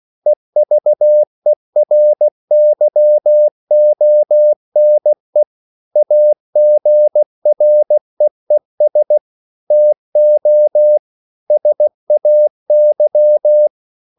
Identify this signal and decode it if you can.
Morse code, heard as a keyed tone, about 16 wpm, 600 Hz. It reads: EVERYONE AGREES TO SAY
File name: CW / Practice / EnglishQuotations0416